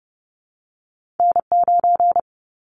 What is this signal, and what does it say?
Morse code, heard as a keyed tone, about 30 wpm, 700 Hz. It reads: N9
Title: Callsign Prefixes for Top 25 DXCC - 30wpm